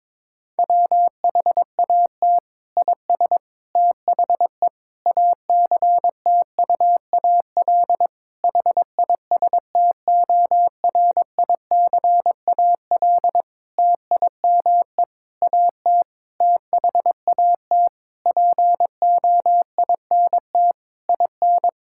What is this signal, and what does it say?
Morse code, heard as a keyed tone, about 22 wpm, 700 Hz. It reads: WHAT IS THE ACTUAL HISTORICAL TIME AT THAT POINT IN
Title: Sentences from Top 1000 Words - Review Part 5 - 22wpm